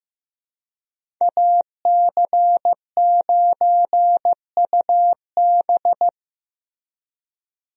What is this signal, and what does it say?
Morse code, heard as a keyed tone, about 15 wpm, 700 Hz. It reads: AC9UB
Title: USA Call Signs - 5 Characters - Repeated 2x - 15wpm